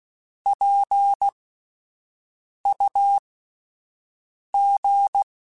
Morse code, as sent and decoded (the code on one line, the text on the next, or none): .--. ..- --.
PUG